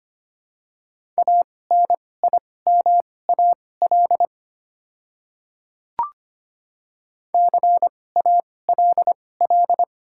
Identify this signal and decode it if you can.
Morse code, heard as a keyed tone, about 25 wpm, spaced as for 18 wpm, 700 Hz. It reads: ANIMAL CALL
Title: 2x Character Spacing - Top 300 Words - 25wpm